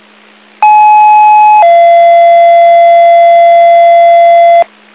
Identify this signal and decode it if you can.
Two-tone page, heard as a one-second tone, then a three-second tone, about 850 then 680 Hz, with noise. It is heard from 0.6 to 4.6 s.